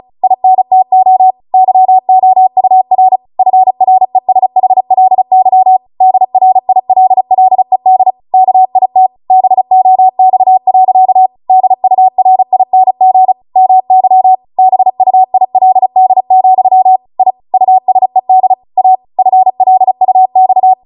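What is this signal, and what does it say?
Morse code, heard as a keyed tone, about 36 wpm, 750 Hz. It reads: INTO YOUR FRESHLY DRILLED KIT BOX. DURING MY BUILD, I USED A FLUX
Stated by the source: Amateur radio Morse code practice archive transmission at 35 wpm.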